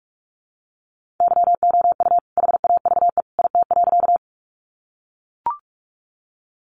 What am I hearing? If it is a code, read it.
Morse code, heard as a keyed tone, about 44 wpm, 700 Hz. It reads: YOU HAVE IT.
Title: Sentences from Top 100 Words 45wpm